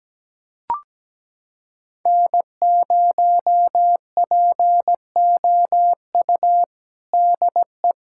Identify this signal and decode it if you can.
Morse code, heard as a keyed tone, about 17 wpm, 700 Hz. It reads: N0POU DE
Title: USA Callsign DE Callsign - 5 Characters - 17wpm